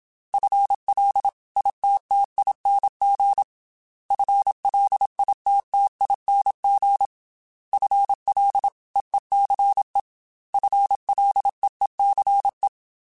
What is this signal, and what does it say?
Morse code, heard as a keyed tone, about 27 wpm, 800 Hz. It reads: FLITTING FLITTING FLEECE FLEECE
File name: FL / FL mp3s / FL Prefixes 3x2 sp @25